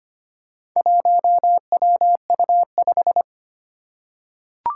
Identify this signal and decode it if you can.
Morse code, heard as a keyed tone, about 25 wpm, 700 Hz. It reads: 1WU5
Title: Four Challenging Letter-Number Combinations 25wpm